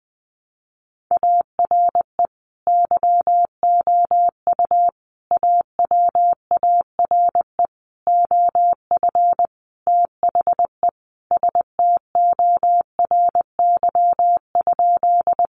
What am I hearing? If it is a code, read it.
Morse code, heard as a keyed tone, about 20 wpm, 700 Hz. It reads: ARE YOU AWARE OF THE STORY?
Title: Sentences from Top 1200 Words 20wpm